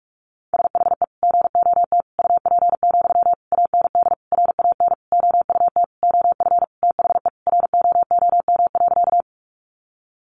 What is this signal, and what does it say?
Morse code, heard as a keyed tone, about 44 wpm, 700 Hz. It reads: SHE GOT UP, AND RAN OUT OF THE ROOM.
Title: Sentences from Top 500 Words 45wpm